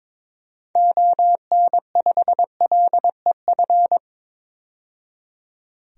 Morse code, heard as a keyed tone, about 21 wpm, 700 Hz. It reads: ON5LEF
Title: DXCC-100 Call Signs - 6 Characters - 22wpm